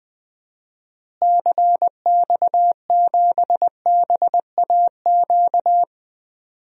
Morse code, heard as a keyed tone, about 20 wpm, 700 Hz. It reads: CX7BAQ
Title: DXCC-100 Call Signs - 6 Characters - Repeated 2x - 20wpm